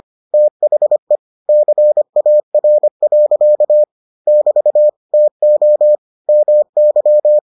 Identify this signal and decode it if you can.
Morse code, heard as a keyed tone, about 25 wpm, 600 Hz. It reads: THE CAR. <BT> TO MY